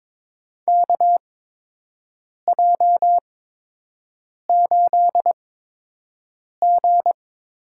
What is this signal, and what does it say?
Morse code, heard as a keyed tone, about 23 wpm, spaced as for 6 wpm, 700 Hz. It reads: KJ8G
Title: USA Call Signs - 4 Characters - 8x Character Spacing - 22wpm